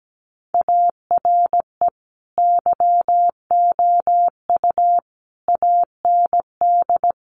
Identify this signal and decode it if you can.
Morse code, heard as a keyed tone, about 17 wpm, 700 Hz. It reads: ARE YOU AND
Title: Sentences from Top 1400 Words 17wpm